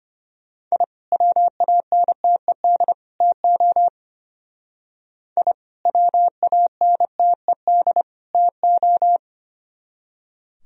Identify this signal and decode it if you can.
Morse code, heard as a keyed tone, about 33 wpm, 700 Hz. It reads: I WANTED TO I WANTED TO